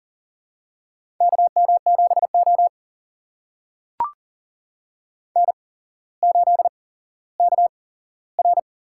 Morse code, heard as a keyed tone, about 40 wpm, spaced as for 17 wpm, 700 Hz. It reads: KM7O N8KR
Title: USA Call Signs - 4 Characters - 8x Character Spacing - 40wpm